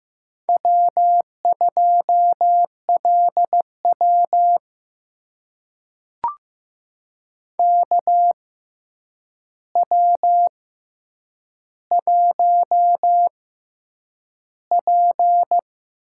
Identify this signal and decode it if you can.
Morse code, heard as a keyed tone, about 15 wpm, spaced as for 8 wpm, 700 Hz. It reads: W2LW KW1P
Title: USA Call Signs - 4 Characters - 6x Character Spacing - 15wpm